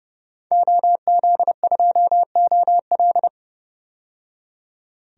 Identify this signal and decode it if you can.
Morse code, heard as a keyed tone, about 30 wpm, 700 Hz. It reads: OZ2OL
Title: DXCC-100 Call Signs - 5 Characters - 30wpm